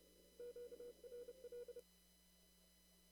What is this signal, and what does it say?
Morse code, heard as a keyed tone, about 29 wpm, 480 Hz. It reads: QRL
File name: CW / kt5j wpx cw-op-k5pi